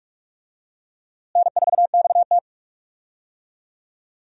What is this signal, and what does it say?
Morse code, heard as a keyed tone, about 44 wpm, 700 Hz. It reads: N4XT
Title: USA Call Signs - 4 Characters - Repeated 2x - 45wpm